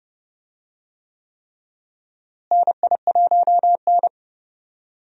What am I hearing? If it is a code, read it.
Morse code, heard as a keyed tone, about 30 wpm, 700 Hz. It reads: NI1N
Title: USA Call Signs - 4 Characters - Repeated 3x - 30wpm